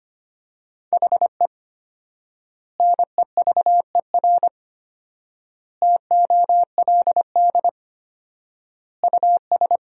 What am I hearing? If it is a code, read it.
Morse code, heard as a keyed tone, about 25 wpm, 700 Hz. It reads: HE NEVER TOLD US
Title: 4x Word-Spacing - Sentences from Top 400 Words - 25wpm